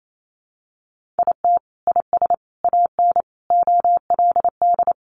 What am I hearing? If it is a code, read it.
Morse code, heard as a keyed tone, about 28 wpm, 700 Hz. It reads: IT IS AN OLD
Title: Sentences from Top 1500 Words 28wpm